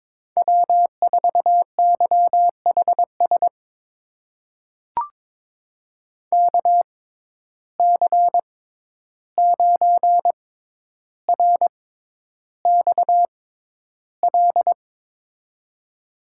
Morse code, heard as a keyed tone, about 22 wpm, spaced as for 11 wpm, 700 Hz. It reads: W4YHS KC9RXL
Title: USA Call Signs - 6x Character Spacing - 22wpm